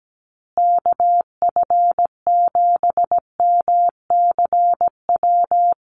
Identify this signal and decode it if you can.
Morse code, heard as a keyed tone, about 17 wpm, 700 Hz. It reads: KF7MCW
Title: USA Call Signs Repeated 2x 17wpm